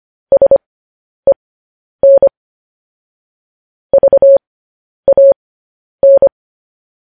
Morse code, heard as a keyed tone, about 25 wpm, spaced as for 10 wpm, 550 Hz. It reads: SEN VAN
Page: news headlines in morse code at 10 wpm